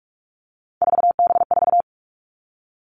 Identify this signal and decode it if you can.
Morse code, heard as a keyed tone, about 46 wpm, 700 Hz. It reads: 4B4
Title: Three Challenging Letter-Number Combinations 45wpm